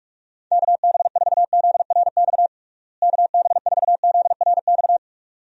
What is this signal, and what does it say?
Morse code, heard as a keyed tone, about 42 wpm, 700 Hz. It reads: KB4ZRX KB4ZRX
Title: USA Call Signs - 6 Characters - Repeated 2x - 45wpm